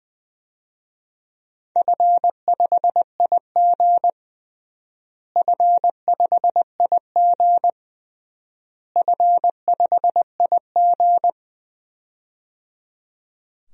Morse code, heard as a keyed tone, about 21 wpm, 700 Hz. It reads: F5IG F5IG F5IG
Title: DXCC-100 Call Signs - 4 Characters - Repeated 3x - 20wpm